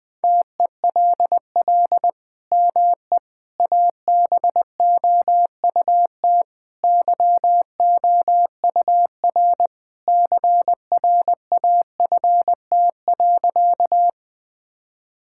Morse code, heard as a keyed tone, about 20 wpm, 700 Hz. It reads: TELL ME ABOUT YOUR CRAFT.
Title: Sentences from Top 1000 Words - Review Part 6 - 20wpm